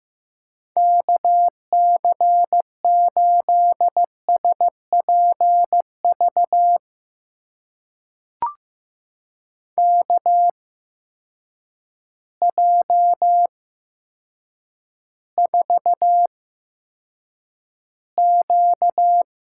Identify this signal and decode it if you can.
Morse code, heard as a keyed tone, about 15 wpm, spaced as for 8 wpm, 700 Hz. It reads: KC8SPV KJ4Q
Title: USA Call Signs - 6 Characters - 8x Character Spacing - 15wpm